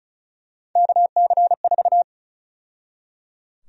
Morse code, heard as a keyed tone, about 36 wpm, 700 Hz. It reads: KC4
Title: USA Call Sign Prefixes - 35wpm